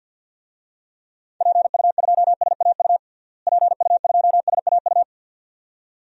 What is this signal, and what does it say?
Morse code, heard as a keyed tone, about 52 wpm, 700 Hz. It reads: PU2SAU PU2SAU
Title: DXCC-100 Call Signs - 6 Characters - Repeated 2x - 50wpm